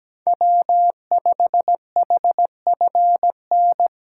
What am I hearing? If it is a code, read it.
Morse code, heard as a keyed tone, about 17 wpm, 700 Hz. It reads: W5HFN
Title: USA Call Signs - 5 Characters - 8x Character Spacing - 17wpm